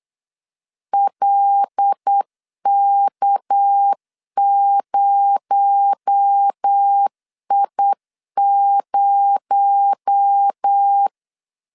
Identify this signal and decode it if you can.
Morse code, heard as a keyed tone, about 8 wpm, 800 Hz. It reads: LK0I0
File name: LEZIONI / 6 PKX0 / 6 PKX0 8wpm 3341